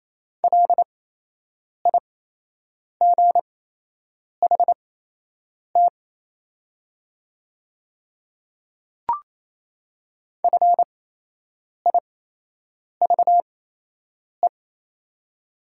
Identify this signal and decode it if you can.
Morse code, heard as a keyed tone, about 28 wpm, spaced as for 8 wpm, 700 Hz. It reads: LIGHT FIVE
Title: 8x Character Spacing - Top 500 Words - 28wpm